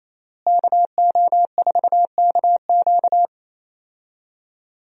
Morse code, heard as a keyed tone, about 28 wpm, 700 Hz. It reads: KO4KQ
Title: USA Call Signs - 5 Characters - Repeated 3x - 28wpm